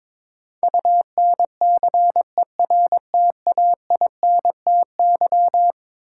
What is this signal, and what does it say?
Morse code, heard as a keyed tone, about 22 wpm, 700 Hz. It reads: UNCERTAINTY
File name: practice / 22wpm / Eleven Letter Words - 22wpm